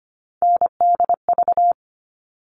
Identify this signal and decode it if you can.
Morse code, heard as a keyed tone, about 25 wpm, 700 Hz. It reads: NDV
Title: All Cut Numbers - 3-Digit 25wpm